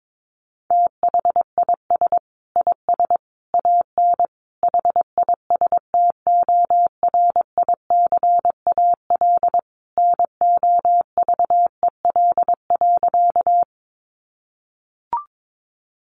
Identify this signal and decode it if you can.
Morse code, heard as a keyed tone, about 22 wpm, 700 Hz. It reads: THIS IS AN HISTORICAL NOVEL.
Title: Sentences from Top 1300 Words 22wpm